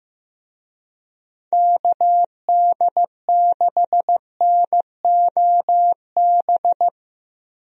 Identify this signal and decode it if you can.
Morse code, heard as a keyed tone, about 16 wpm, 700 Hz. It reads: KD6NOB
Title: USA Call Signs - 6 Characters - Repeated 3x - 15wpm